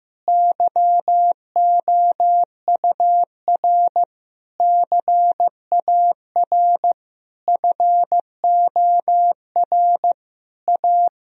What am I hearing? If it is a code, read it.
Morse code, heard as a keyed tone, about 16 wpm, 700 Hz. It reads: YOUR CAR FOR A
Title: Sentences from Top 1000 Words - Review Part 1 - 15wpm